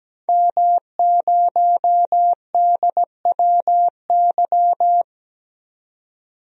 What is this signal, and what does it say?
Morse code, heard as a keyed tone, about 17 wpm, 700 Hz. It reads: M0DWY
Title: DXCC-100 Call Signs - 5 Characters - 17wpm